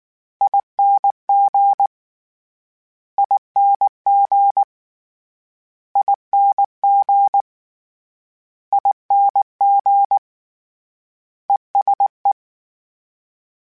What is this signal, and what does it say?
Morse code, heard as a keyed tone, about 19 wpm, 800 Hz. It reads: ING ING ING ING ESE
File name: Short Runs / Short Runs-mp3s / ESE ING @18